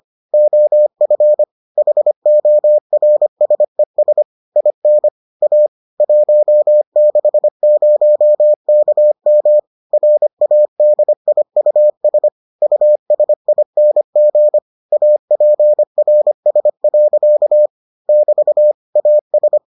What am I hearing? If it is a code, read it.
Morse code, heard as a keyed tone, about 25 wpm, 600 Hz. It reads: OF HORSES IN A 160KM RADIUS USING APRS. <BT> AS